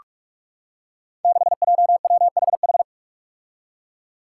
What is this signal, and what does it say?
Morse code, heard as a keyed tone, about 43 wpm, 700 Hz. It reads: 6JWHH
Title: Five Challenging Letter-Number Combinations 45wpm